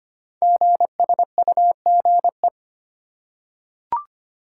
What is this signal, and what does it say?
Morse code, heard as a keyed tone, about 25 wpm, 700 Hz. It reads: GSUGE
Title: Code Groups - 5 Letters - 25wpm